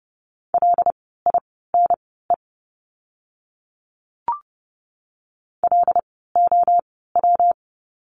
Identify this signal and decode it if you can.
Morse code, heard as a keyed tone, about 30 wpm, spaced as for 17 wpm, 700 Hz. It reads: LINE LOW
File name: course / 30:10wpm / 115 Lesson 19f - Letter w - Words 30wpm